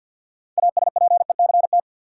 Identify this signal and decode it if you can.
Morse code, heard as a keyed tone, about 52 wpm, 700 Hz. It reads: ASPECT
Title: Sentences from Top 1300 Words - New Word Drill Alt - 50wpm